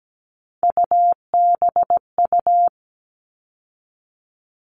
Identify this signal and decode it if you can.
Morse code, heard as a keyed tone, about 17 wpm, 700 Hz. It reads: UBU
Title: All Cut Numbers - 3-Digit 17wpm